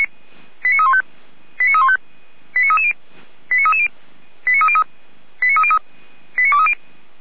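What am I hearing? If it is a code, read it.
Five-tone selective call (ZVEI), heard as quick runs of five tones, tones of about 70 ms. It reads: 89315 89315 89330 89330 89303 89303 89230
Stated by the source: SDR-received ZVEI five-tone selective call.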